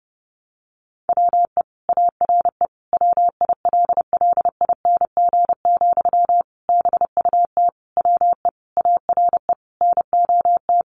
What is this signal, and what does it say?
Morse code, heard as a keyed tone, about 30 wpm, 700 Hz. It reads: WE ARE WILLING, BUT WE ARE NOT
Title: Sentences from Top 1300 Words 30wpm